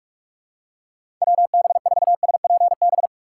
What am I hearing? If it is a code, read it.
Morse code, heard as a keyed tone, about 46 wpm, 700 Hz. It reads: WB4SPB
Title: K1USN SST QSO Practice - 45wpm